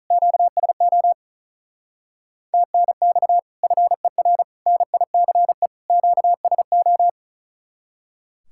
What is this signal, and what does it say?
Morse code, heard as a keyed tone, about 46 wpm, 700 Hz. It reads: QSO TNX FER NICE QSO